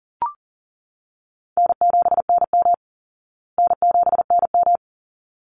Morse code, heard as a keyed tone, about 40 wpm, 700 Hz. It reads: N7NM N7NM
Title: Call Signs Repeated 3x - 40wpm